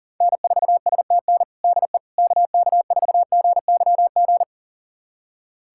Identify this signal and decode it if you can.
Morse code, heard as a keyed tone, about 40 wpm, 700 Hz. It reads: N4STN DE KK4GYG
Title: USA Callsign DE Callsign 40wpm